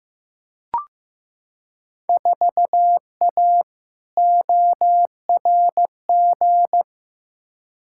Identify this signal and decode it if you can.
Morse code, heard as a keyed tone, about 15 wpm, 700 Hz. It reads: 4A ORG
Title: Field Day Exchanges - 15wpm